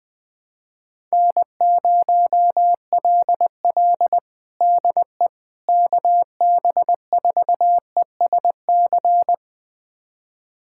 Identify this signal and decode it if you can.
Morse code, heard as a keyed tone, about 20 wpm, 690 Hz. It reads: N0LL DE KB4ESC
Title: USA Callsign DE Callsign 20wpm